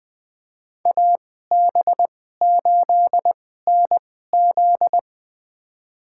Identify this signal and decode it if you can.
Morse code, heard as a keyed tone, about 21 wpm, spaced as for 15 wpm, 700 Hz. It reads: AB8NZ
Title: USA Call Signs - 5 Characters - 2x Character Spacing - 20wpm